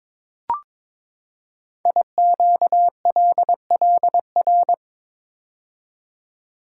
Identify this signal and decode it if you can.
Morse code, heard as a keyed tone, about 22 wpm, 700 Hz. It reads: IQLLR
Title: Code Groups - 5 Letters - 22wpm